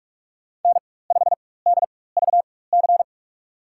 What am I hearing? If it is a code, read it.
Morse code, heard as a keyed tone, about 46 wpm, spaced as for 21 wpm, 700 Hz. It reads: N5DVC
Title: USA Call Signs - 5 Characters - 4x Character Spacing - 45wpm